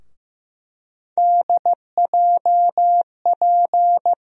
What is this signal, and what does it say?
Morse code, heard as a keyed tone, about 15 wpm, 700 Hz. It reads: DJP
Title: USA Call Sign Postfixes - 3 Letters 15wpm